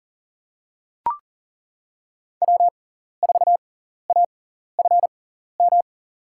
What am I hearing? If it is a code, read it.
Morse code, heard as a keyed tone, about 40 wpm, spaced as for 14 wpm, 700 Hz. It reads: W4AFM
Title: USA Call Signs - 5 Characters - 6x Character Spacing - 40wpm